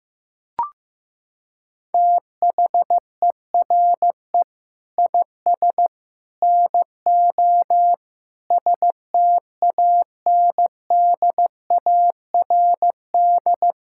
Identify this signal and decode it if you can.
Morse code, heard as a keyed tone, about 15 wpm, 700 Hz. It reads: THERE IS NO STANDARD
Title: Sentences from Top 1000 Words - Review Part 6 - 15wpm